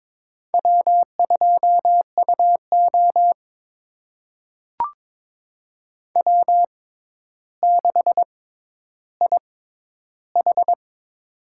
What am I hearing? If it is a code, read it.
Morse code, heard as a keyed tone, about 22 wpm, spaced as for 11 wpm, 700 Hz. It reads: W2UO W6IH
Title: USA Call Signs - 4 Characters - 6x Character Spacing - 22wpm